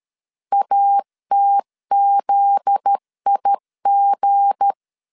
Morse code, heard as a keyed tone, about 13 wpm, 800 Hz. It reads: ATZIG
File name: LEZIONI / 7 GF9Z / 7 GF9Z 12wpm 2147